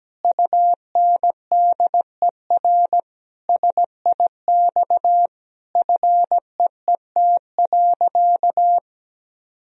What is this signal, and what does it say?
Morse code, heard as a keyed tone, about 17 wpm, 700 Hz. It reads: UNDER SIX FEET.
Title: Sentences from Top 1000 Words - Review Part 2 - 17wpm